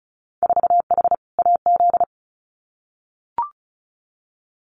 Morse code, heard as a keyed tone, about 36 wpm, 700 Hz. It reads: 4H AZ
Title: Winter Field Day Exchanges Encore 35wpm